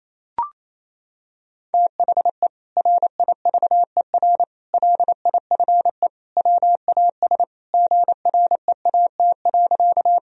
Morse code, heard as a keyed tone, about 28 wpm, 700 Hz. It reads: THE RIVER LIFE WAS GREAT.
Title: Sentences from Top 1000 Words - Review Part 1 - 28wpm